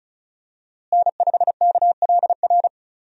Morse code, heard as a keyed tone, about 34 wpm, 700 Hz. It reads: N5KLR
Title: USA Call Signs - 5 Characters - Repeated 3x - 35wpm